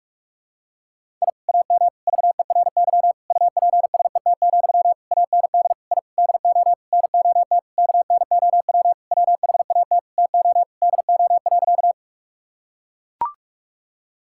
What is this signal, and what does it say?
Morse code, heard as a keyed tone, about 41 wpm, 700 Hz. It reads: I AM VERY UPSET, AND I DO NOT KNOW WHAT TO DO.
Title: Sentences from Top 1000 Words - Review Part 5 - 45wpm